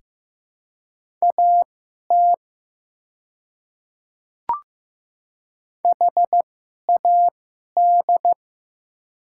Morse code, heard as a keyed tone, about 15 wpm, spaced as for 11 wpm, 700 Hz. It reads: AT HAD
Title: 2x Character Spacing - Top 200 Words - 15wpm